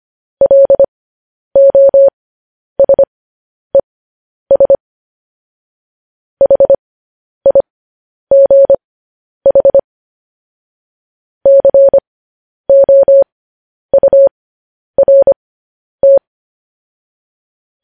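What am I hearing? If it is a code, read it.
Morse code, heard as a keyed tone, about 25 wpm, spaced as for 10 wpm, 550 Hz. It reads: LOSES HIGH COURT
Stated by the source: Morse code news headlines at 10.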